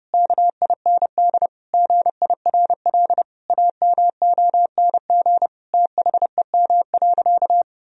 Morse code, heard as a keyed tone, about 30 wpm, 700 Hz. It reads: KIND GIRL AMONG THEM.
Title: Sentences from Top 1000 Words - Review Part 3 - 30wpm